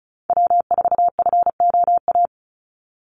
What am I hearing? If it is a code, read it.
Morse code, heard as a keyed tone, about 34 wpm, 700 Hz. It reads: W4FOA
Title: Call Signs Repeated 3x - 35wpm